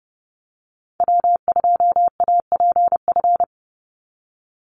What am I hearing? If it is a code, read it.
Morse code, heard as a keyed tone, about 30 wpm, 700 Hz. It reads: W2APF
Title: Call Signs Repeated 3x - 30wpm